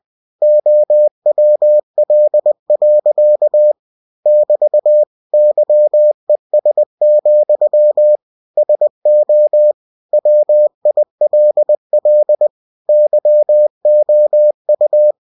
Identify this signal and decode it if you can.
Morse code, heard as a keyed tone, about 20 wpm, 600 Hz. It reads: OWL. <BT> YES, SO WILL YOU
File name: cw-books / winnie-the-pooh / winnie-20wpm-page-0102